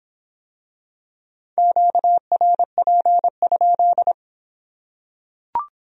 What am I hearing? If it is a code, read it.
Morse code, heard as a keyed tone, about 26 wpm, 700 Hz. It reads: QRP?